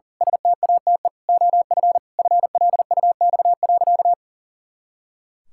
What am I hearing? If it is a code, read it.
Morse code, heard as a keyed tone, about 37 wpm, 700 Hz. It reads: STATE OF FLUX.
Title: Pangram Challenge v2 - 40wpm